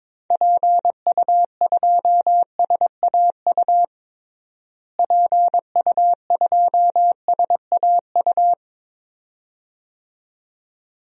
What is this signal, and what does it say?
Morse code, heard as a keyed tone, about 22 wpm, 700 Hz. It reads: PU2SAU PU2SAU
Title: DXCC-100 Call Signs - 6 Characters - Repeated 2x - 22wpm